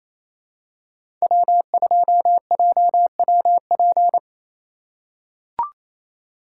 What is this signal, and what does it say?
Morse code, heard as a keyed tone, about 27 wpm, 700 Hz. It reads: W2JWP
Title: USA Call Signs - 4x Character Spacing - 28wpm